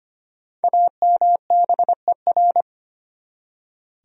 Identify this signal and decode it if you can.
Morse code, heard as a keyed tone, about 25 wpm, 700 Hz. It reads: AMBER
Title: USA Common Names - Max 5 Letters - 25wpm